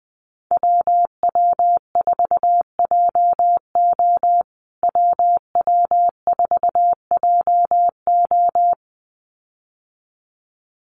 Morse code, heard as a keyed tone, about 20 wpm, 700 Hz. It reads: WW4JO WW4JO
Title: USA Call Signs Repeated 2x 20wpm